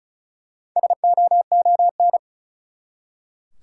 Morse code, heard as a keyed tone, about 36 wpm, 700 Hz. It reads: SOON
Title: Four Letter Words - 35wpm